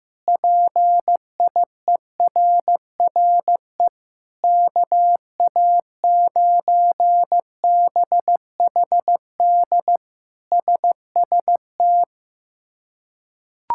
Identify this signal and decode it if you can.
Morse code, heard as a keyed tone, about 15 wpm, 700 Hz. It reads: PIERRE KA9BHD SST
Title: K1USN SST QSO Practice - 15wpm